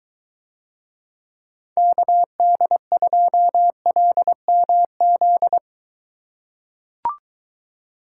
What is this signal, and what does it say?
Morse code, heard as a keyed tone, about 23 wpm, 700 Hz. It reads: KD2LMZ